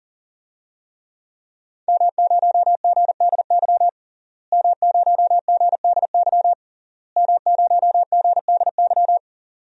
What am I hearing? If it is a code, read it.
Morse code, heard as a keyed tone, about 39 wpm, 700 Hz. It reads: M0GDY M0GDY M0GDY
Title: DXCC-100 Call Signs - 5 Characters - Repeated 3x - 40wpm